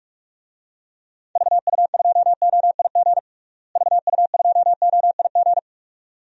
Morse code, heard as a keyed tone, about 46 wpm, 700 Hz. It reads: VU2OIG VU2OIG
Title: DXCC-100 Call Signs - 6 Characters - Repeated 2x - 45wpm